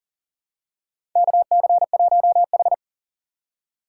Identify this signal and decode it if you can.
Morse code, heard as a keyed tone, about 40 wpm, 700 Hz. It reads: KC1H
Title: USA Call Signs - 6x Character Spacing - 40wpm